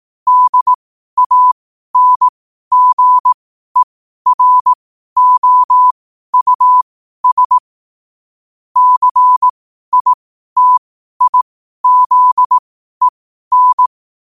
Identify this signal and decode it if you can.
Morse code, heard as a keyed tone, about 18 wpm, spaced as for 12 wpm, 1000 Hz. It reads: DANGEROUS CITIZEN
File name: qotdfiles / 12wpm